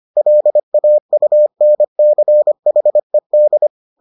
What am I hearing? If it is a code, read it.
Morse code, heard as a keyed tone, about 25 wpm, 600 Hz. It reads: LAUNCHED